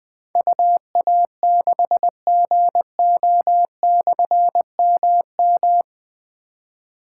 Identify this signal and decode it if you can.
Morse code, heard as a keyed tone, about 20 wpm, 700 Hz. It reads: UA6GO/MM
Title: Difficult Call Signs 20wpm